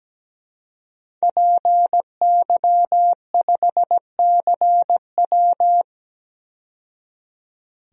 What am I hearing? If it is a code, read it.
Morse code, heard as a keyed tone, about 17 wpm, 700 Hz. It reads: PY5CW
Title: DXCC-100 Call Signs - 5 Characters - 17wpm